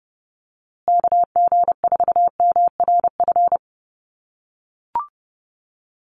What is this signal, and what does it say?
Morse code, heard as a keyed tone, about 30 wpm, 700 Hz. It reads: KG4MRF
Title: USA Call Signs Repeated 2x 30wpm